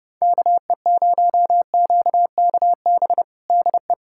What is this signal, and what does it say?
Morse code, heard as a keyed tone, about 30 wpm, 700 Hz. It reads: KE0QKB DE
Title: USA Callsign DE Callsign - 6 Characters - 30wpm